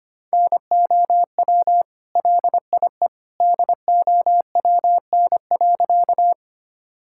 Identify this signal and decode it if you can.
Morse code, heard as a keyed tone, about 25 wpm, 700 Hz. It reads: NOW LIE DOWN.
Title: Sentences from Top 1600 Words 25wpm